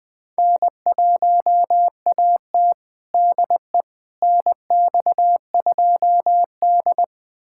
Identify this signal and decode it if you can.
Morse code, heard as a keyed tone, about 20 wpm, 700 Hz. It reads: N1AT DE NX2D
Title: USA Callsign DE Callsign - 4 Characters - 20wpm